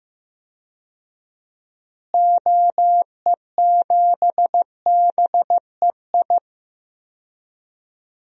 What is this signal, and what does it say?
Morse code, heard as a keyed tone, about 15 wpm, 700 Hz. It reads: OE7BEI
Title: DXCC-100 Call Signs - 6 Characters - Repeated 2x - 15wpm